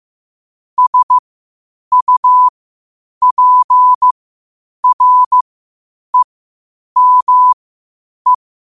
Morse code, heard as a keyed tone, about 15 wpm, spaced as for 8 wpm, 1000 Hz. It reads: SUPREME
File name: qotdfiles / 8wpm